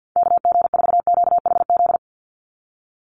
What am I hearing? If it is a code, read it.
Morse code, heard as a keyed tone, about 50 wpm, 700 Hz. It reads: KG4QHZ
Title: USA Call Signs 50wpm